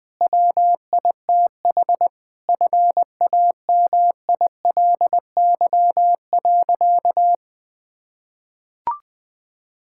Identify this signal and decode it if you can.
Morse code, heard as a keyed tone, about 20 wpm, 700 Hz. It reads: WITH FAMILY.
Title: Difficult Sentences with Hints - 20wpm